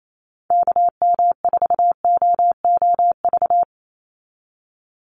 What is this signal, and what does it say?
Morse code, heard as a keyed tone, about 28 wpm, 700 Hz. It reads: KM4OOV
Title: USA Call Signs 28wpm